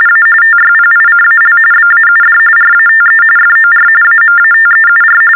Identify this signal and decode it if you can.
TTY/TDD text telephone tones, about 45.45 baud. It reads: A TYPICAL BAUD RATE FOR AMATEUR
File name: teletype-RTTY-45